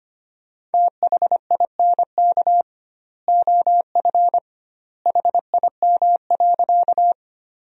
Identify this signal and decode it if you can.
Morse code, heard as a keyed tone, about 25 wpm, 700 Hz. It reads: THINK OF HIM.
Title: 2x Word-Spacing - Sentences from Top 200 Words - 25wpm